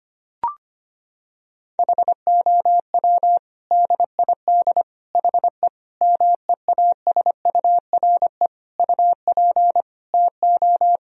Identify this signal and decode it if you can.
Morse code, heard as a keyed tone, about 25 wpm, 700 Hz. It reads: HOW DID HE MEASURE UP TO
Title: Sentences from Top 1000 Words - Review Part 2 - 25wpm